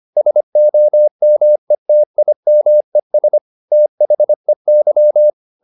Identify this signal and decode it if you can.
Morse code, heard as a keyed tone, about 25 wpm, 600 Hz. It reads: SOMETIMES THEY